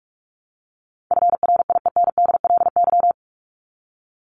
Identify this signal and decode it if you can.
Morse code, heard as a keyed tone, about 44 wpm, 700 Hz. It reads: FRIENDLY
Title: LY Suffix - Top 100 Words with Base Words 45wpm